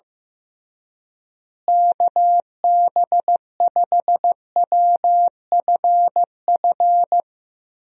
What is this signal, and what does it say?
Morse code, heard as a keyed tone, about 16 wpm, 700 Hz. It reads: KB5WFF
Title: USA Call Signs - 6 Characters - Repeated 3x - 15wpm